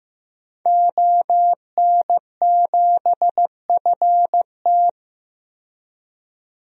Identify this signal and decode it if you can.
Morse code, heard as a keyed tone, about 15 wpm, 700 Hz. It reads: ON7FT
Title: DXCC-100 Call Signs - 5 Characters - 15wpm